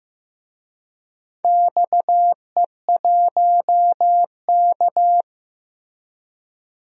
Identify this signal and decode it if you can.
Morse code, heard as a keyed tone, about 15 wpm, 700 Hz. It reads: XE1K
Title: DXCC-100 Call Signs - 4 Characters - Repeated 2x - 15wpm